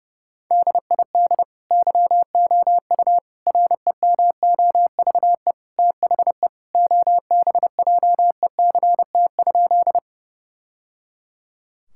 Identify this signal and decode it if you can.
Morse code, heard as a keyed tone, about 30 wpm, 700 Hz. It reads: DID YOU REMOVE THE OBJECT?
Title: Sentences from Top 1000 Words - Review Part 5 - 30wpm